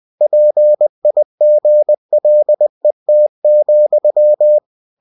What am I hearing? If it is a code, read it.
Morse code, heard as a keyed tone, about 20 wpm, 600 Hz. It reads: PIGLET,
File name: cw-books / winnie-the-pooh / winnie-20wpm-page-0048